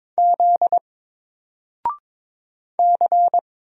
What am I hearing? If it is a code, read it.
Morse code, heard as a keyed tone, about 22 wpm, 700 Hz. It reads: Z C
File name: Single-Letter-Number-22wpm